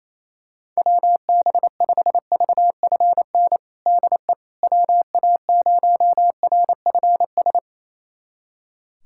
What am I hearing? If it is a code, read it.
Morse code, heard as a keyed tone, about 28 wpm, 700 Hz. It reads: WB5VFN DE WA0RFS
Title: USA Callsign DE Callsign - 6 Characters - 28wpm